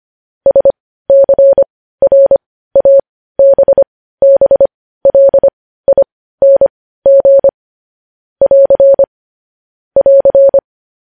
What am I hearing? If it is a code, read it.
Morse code, heard as a keyed tone, about 25 wpm, spaced as for 15 wpm, 550 Hz. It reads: SCRABBLING <AR> <AR>